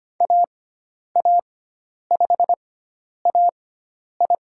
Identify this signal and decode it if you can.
Morse code, heard as a keyed tone, about 25 wpm, spaced as for 10 wpm, 700 Hz. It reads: AA5AI
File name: Lesson 16g - Letter c - Call Signs 25wpm